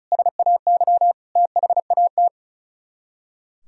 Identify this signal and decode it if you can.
Morse code, heard as a keyed tone, about 43 wpm, 700 Hz. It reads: SAY THAT
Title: Speed-Racing - Sets of 5 Words 35wpm